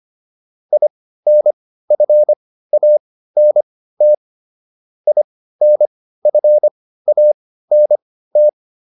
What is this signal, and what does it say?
Morse code, heard as a keyed tone, about 25 wpm, spaced as for 15 wpm, 600 Hz. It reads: INFANT INFANT